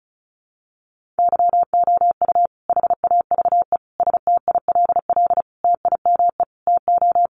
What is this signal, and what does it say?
Morse code, heard as a keyed tone, about 34 wpm, 700 Hz. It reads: YOU HAVE STILL TIME TO
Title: Sentences from Top 600 Words 35wpm